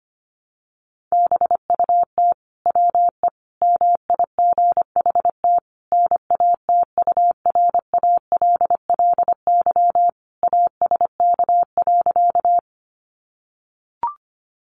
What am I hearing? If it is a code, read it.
Morse code, heard as a keyed tone, about 25 wpm, 700 Hz. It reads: BUT WE MIGHT NATURALLY ASK.
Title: Sentences from Top 1300 Words 25wpm